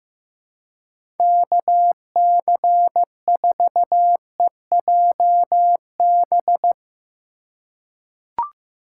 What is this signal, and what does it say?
Morse code, heard as a keyed tone, about 15 wpm, 700 Hz. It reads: KC4EJB
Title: USA Call Signs - 3x Character Spacing - 15wpm